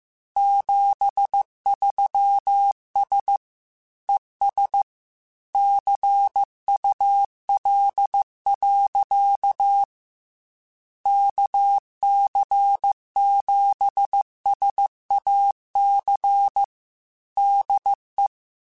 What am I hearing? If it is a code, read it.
Morse code, heard as a keyed tone, about 15 wpm, 800 Hz. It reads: 73S ES CUL. KC7SAC DE
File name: QSOs / QSOs-mp3s / kc7sac@13